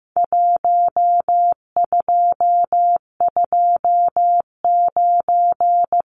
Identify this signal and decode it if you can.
Morse code, heard as a keyed tone, about 15 wpm, 700 Hz. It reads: 1229
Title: Numbers - Contest Style - 4-Digit 15wpm